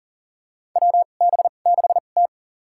Morse code, heard as a keyed tone, about 40 wpm, spaced as for 29 wpm, 700 Hz. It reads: WB6T
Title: USA Call Signs - 2x Character Spacing - 40wpm